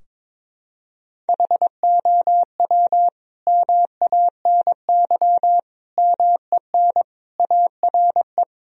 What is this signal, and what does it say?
Morse code, heard as a keyed tone, about 22 wpm, 700 Hz. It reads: HOW MANY MEN ARE
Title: Sentences from Top 1000 Words - Review Part 3 - 22wpm